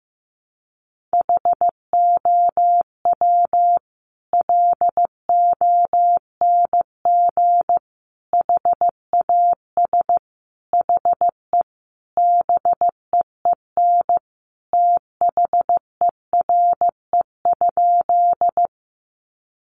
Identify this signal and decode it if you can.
Morse code, heard as a keyed tone, about 15 wpm, 700 Hz. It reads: HOW LONG HAS HE BEEN THERE?
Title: Sentences from Top 100 Words 15wpm